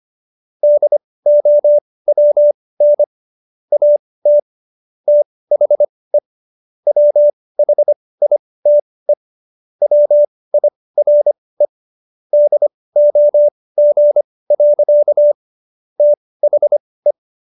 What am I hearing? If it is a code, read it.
Morse code, heard as a keyed tone, about 25 wpm, spaced as for 18 wpm, 590 Hz. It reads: DOWN AT THE WHITE WIRE DOG. THE